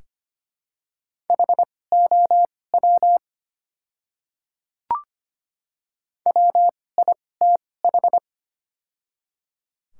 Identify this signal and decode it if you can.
Morse code, heard as a keyed tone, about 25 wpm, spaced as for 18 wpm, 700 Hz. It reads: HOW WITH
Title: 2x Character Spacing - Top 300 Words - 25wpm